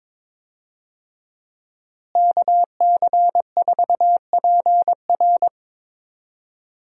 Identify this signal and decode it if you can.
Morse code, heard as a keyed tone, about 22 wpm, 700 Hz. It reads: KC4PR
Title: USA Call Signs - 5 Characters - Repeated 2x - 22wpm